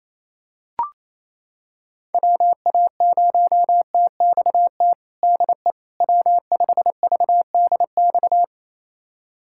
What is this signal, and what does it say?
Morse code, heard as a keyed tone, about 28 wpm, 700 Hz. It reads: WA0TXT DE W5VDX
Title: USA Callsign DE Callsign 28wpm